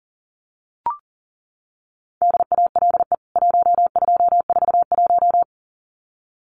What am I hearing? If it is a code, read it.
Morse code, heard as a keyed tone, about 40 wpm, 700 Hz. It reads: DALE 1241
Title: CWT Exchange 40wpm